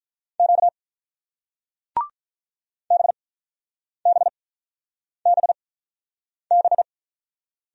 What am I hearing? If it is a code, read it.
Morse code, heard as a keyed tone, about 45 wpm, spaced as for 9 wpm, 700 Hz. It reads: Q BBBB